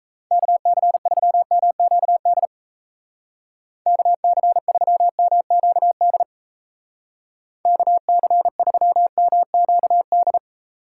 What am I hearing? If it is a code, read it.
Morse code, heard as a keyed tone, about 37 wpm, 700 Hz. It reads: KC3MQD KC3MQD KC3MQD